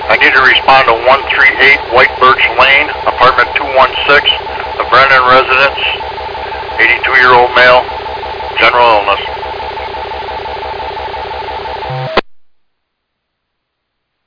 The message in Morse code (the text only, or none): none